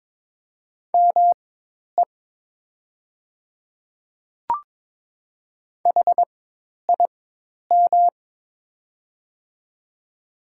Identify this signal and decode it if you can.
Morse code, heard as a keyed tone, about 22 wpm, spaced as for 10 wpm, 700 Hz. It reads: ME HIM
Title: 4x Character Spacing - Top 200 Words - 22wpm